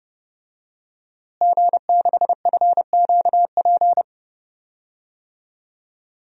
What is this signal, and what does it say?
Morse code, heard as a keyed tone, about 30 wpm, 700 Hz. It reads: G6FQP
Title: DXCC-100 Call Signs - 5 Characters - Repeated 3x - 30wpm